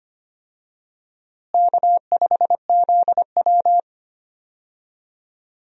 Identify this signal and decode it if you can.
Morse code, heard as a keyed tone, about 25 wpm, 700 Hz. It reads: K5ZW
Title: USA Call Signs - 4 Characters - Repeated 3x - 25wpm